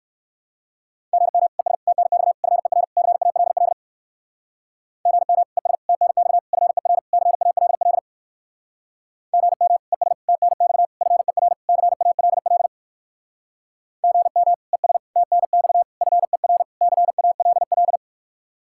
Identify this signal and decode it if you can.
Morse code, heard as a keyed tone, about 67 wpm, 700 Hz. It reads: GM ES TNX FER CALL GM ES TNX FER CALL GM ES TNX FER CALL GM ES TNX FER CALL